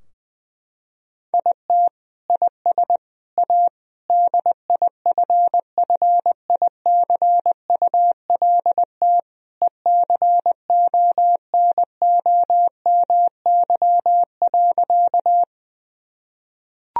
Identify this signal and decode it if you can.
Morse code, heard as a keyed tone, about 21 wpm, 700 Hz. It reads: IT IS A DIFFICULT ECONOMY.
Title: Sentences from Top 1000 Words - Review Part 3 - 20wpm